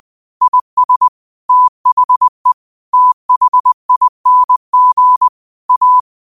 Morse code, heard as a keyed tone, about 20 wpm, 1000 Hz. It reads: IS THE THING A
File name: qotdfiles / 20wpm